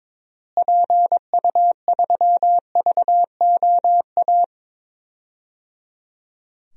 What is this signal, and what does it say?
Morse code, heard as a keyed tone, about 22 wpm, 700 Hz. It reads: PU3VOA
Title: DXCC-100 Call Signs - 6 Characters - Repeated 3x - 22wpm